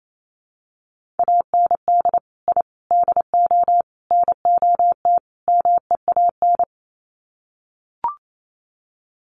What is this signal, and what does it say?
Morse code, heard as a keyed tone, about 28 wpm, 700 Hz. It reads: AND I DO NOT MEAN
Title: Sets of 5 Words Encore2 28wpm